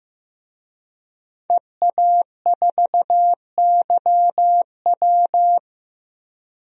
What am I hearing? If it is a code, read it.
Morse code, heard as a keyed tone, about 15 wpm, 700 Hz. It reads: EA4YW
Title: DXCC-100 Call Signs - 5 Characters - 15wpm